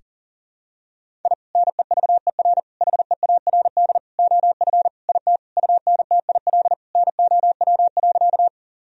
Morse code, heard as a keyed tone, about 37 wpm, 700 Hz. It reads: I NEVER HEARD OF IT UNTIL NOW.
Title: Sentences from Top 1000 Words - Review Part 1 - 40wpm